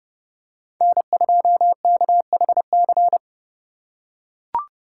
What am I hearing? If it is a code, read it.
Morse code, heard as a keyed tone, about 30 wpm, 700 Hz. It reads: N2KHC
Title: USA Call Signs - 2x Character Spacing - 30wpm